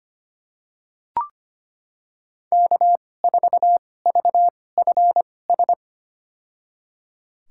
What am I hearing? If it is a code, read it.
Morse code, heard as a keyed tone, about 25 wpm, spaced as for 18 wpm, 700 Hz. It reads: K4VFS
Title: USA Call Signs - 5 Characters - 2x Character Spacing - 25wpm